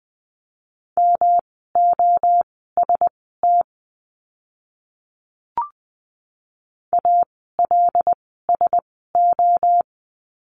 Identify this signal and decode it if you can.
Morse code, heard as a keyed tone, about 20 wpm, spaced as for 14 wpm, 700 Hz. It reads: MOST ALSO
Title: 139 Lesson 23f - Letter f - Words 20wpm